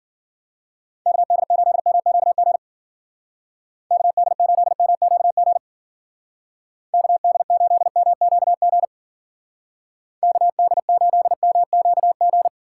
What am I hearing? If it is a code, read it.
Morse code, heard as a keyed tone, about 48 wpm, 700 Hz. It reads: KD8MQG KD8MQG KD8MQG KD8MQG
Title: Speed Racing - USA 2x3 Call Signs 40wpm